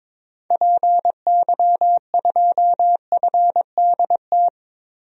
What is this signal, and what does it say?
Morse code, heard as a keyed tone, about 22 wpm, 700 Hz. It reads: PY2FDT
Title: DXCC-100 Call Signs - 6 Characters - Repeated 2x - 22wpm